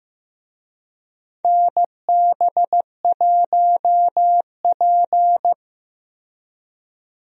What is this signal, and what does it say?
Morse code, heard as a keyed tone, about 16 wpm, 700 Hz. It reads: NB1P
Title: USA Call Signs - 4 Characters - Repeated 3x - 15wpm